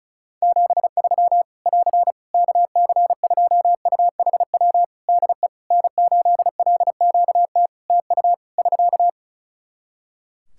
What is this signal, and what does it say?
Morse code, heard as a keyed tone, about 36 wpm, 700 Hz. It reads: 73 <AR> KC2UHW DE N8LQT TU <SK>
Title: Practice QSO - Exchange 4 - Ending - 35wpm